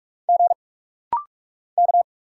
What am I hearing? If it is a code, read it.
Morse code, heard as a keyed tone, about 46 wpm, 700 Hz. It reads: G K
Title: Single Letters - Rapid-Fire 45wpm